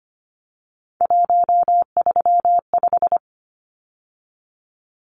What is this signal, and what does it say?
Morse code, heard as a keyed tone, about 25 wpm, 690 Hz. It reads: 135
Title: Numbers - 3-Digit 25wpm